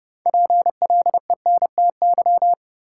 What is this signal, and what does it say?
Morse code, heard as a keyed tone, about 31 wpm, 700 Hz. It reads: PLENTY
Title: Sentences from Top 1300 Words - New Word Drill Alt - 30wpm